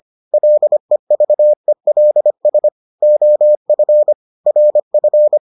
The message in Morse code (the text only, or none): LEVELS OF RF